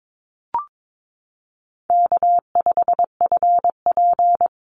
Morse code, heard as a keyed tone, about 22 wpm, 700 Hz. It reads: K5FP
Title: Easy Call Signs Repeated 2x - 22wpm